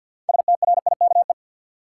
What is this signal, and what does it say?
Morse code, heard as a keyed tone, about 53 wpm, 700 Hz. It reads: STRIKE
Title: Six Letter Words - 50wpm